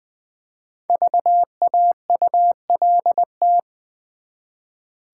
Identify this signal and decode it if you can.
Morse code, heard as a keyed tone, about 20 wpm, 690 Hz. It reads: VAULT
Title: Easy Words with Hints - 20wpm